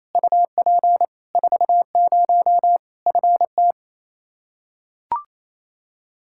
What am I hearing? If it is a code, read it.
Morse code, heard as a keyed tone, about 28 wpm, 700 Hz. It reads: UP 40 FT
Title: Practice QSO - Exchange 2 - Info - 28wpm